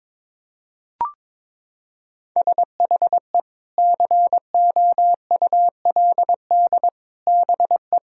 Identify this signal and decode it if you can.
Morse code, heard as a keyed tone, about 22 wpm, 700 Hz. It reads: SHE COULD BE
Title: Sentences from Top 1000 Words - Review Part 4 - 22wpm